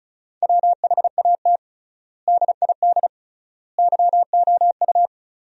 Morse code, heard as a keyed tone, about 36 wpm, 700 Hz. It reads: WHAT DID YOU
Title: 3x Word-Spacing - Sentences from Top 300 Words - 35wpm